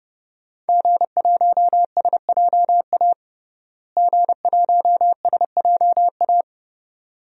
Morse code, heard as a keyed tone, about 32 wpm, 700 Hz. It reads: G1SJA G1SJA
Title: DXCC-100 Call Signs - 5 Characters - Repeated 2x - 30wpm